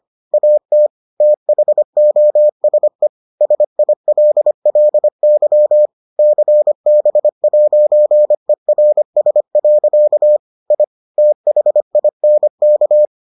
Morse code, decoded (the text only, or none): AT THOSE SILLY CB'ERS. I THINK